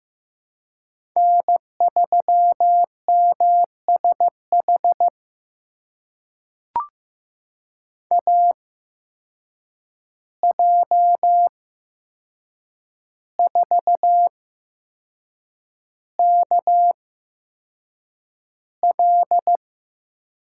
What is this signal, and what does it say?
Morse code, heard as a keyed tone, about 15 wpm, spaced as for 6 wpm, 700 Hz. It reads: N3MSH AJ4KL
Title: USA Call Signs - 5 Characters - 8x Character Spacing - 15wpm